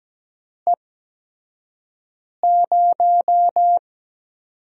Morse code, heard as a keyed tone, about 17 wpm, spaced as for 5 wpm, 700 Hz. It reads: E0
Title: USA Call Signs - 5 Characters - 8x Character Spacing - 17wpm